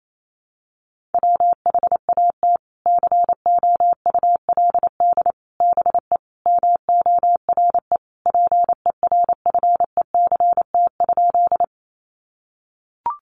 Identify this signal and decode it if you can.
Morse code, heard as a keyed tone, about 28 wpm, 700 Hz. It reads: WHAT COULD BE MORE PERFECT?
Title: Sentences from Top 1000 Words 28wpm